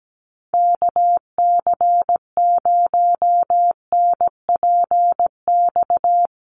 Morse code, heard as a keyed tone, about 17 wpm, 700 Hz. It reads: KC0NPX
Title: USA Call Signs Repeated 2x 17wpm